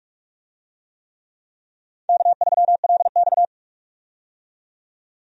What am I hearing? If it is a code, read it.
Morse code, heard as a keyed tone, about 46 wpm, 700 Hz. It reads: K3LX
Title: USA Call Signs - 4 Characters - 45wpm